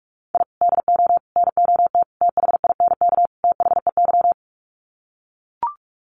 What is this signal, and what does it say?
Morse code, heard as a keyed tone, about 44 wpm, 700 Hz. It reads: I DO NOT THINK THEY
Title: Sets of 5 Words 45wpm